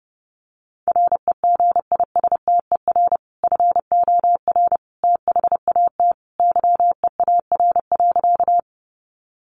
Morse code, heard as a keyed tone, about 30 wpm, 700 Hz. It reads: REGISTER FOR THAT YEAR.
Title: Sentences from Top 1200 Words 30wpm